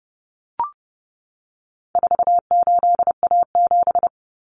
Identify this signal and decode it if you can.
Morse code, heard as a keyed tone, about 30 wpm, 700 Hz. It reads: 48A7
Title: Common Cut Numbers - 4-Digit 30wpm